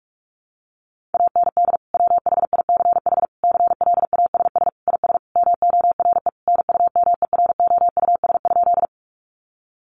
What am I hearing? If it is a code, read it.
Morse code, heard as a keyed tone, about 44 wpm, 700 Hz. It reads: AND WHICH CLASS IS MORE NUMEROUS?
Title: Sentences from Top 1400 Words 45wpm